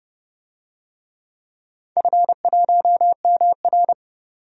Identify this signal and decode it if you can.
Morse code, heard as a keyed tone, about 30 wpm, 700 Hz. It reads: F1MR
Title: DXCC-100 Call Signs - 4 Characters - 30wpm